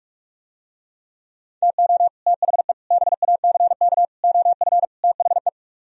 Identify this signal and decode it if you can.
Morse code, heard as a keyed tone, about 44 wpm, 700 Hz. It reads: TO THE BACK OF THE